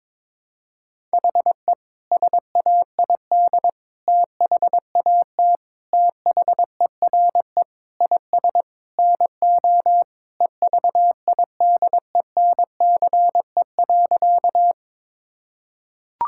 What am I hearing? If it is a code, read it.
Morse code, heard as a keyed tone, about 22 wpm, 700 Hz. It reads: HE SAID THAT THERE IS NO EVIDENCE.
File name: practice / 22wpm / Sentences with 4-Word Phrases - 22wpm